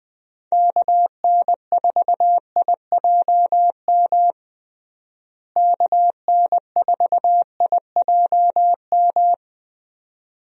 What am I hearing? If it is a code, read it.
Morse code, heard as a keyed tone, about 21 wpm, 700 Hz. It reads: KN4IJM KN4IJM
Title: USA Call Signs - 6 Characters - Repeated 2x - 20wpm